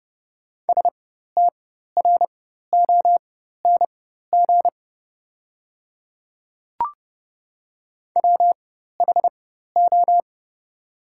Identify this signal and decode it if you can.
Morse code, heard as a keyed tone, about 29 wpm, spaced as for 14 wpm, 690 Hz. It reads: STRONG WHO
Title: 4x Character Spacing - Top 500 Words - 30wpm